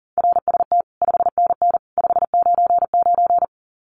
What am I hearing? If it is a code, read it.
Morse code, heard as a keyed tone, about 40 wpm, 700 Hz. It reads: RST 5NN 599
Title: Common QSO Phrases 40wpm